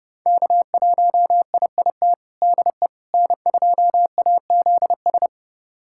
Morse code, heard as a keyed tone, about 30 wpm, 700 Hz. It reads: K1IIT DE N2AZS
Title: USA Callsign DE Callsign - 5 Characters - 30wpm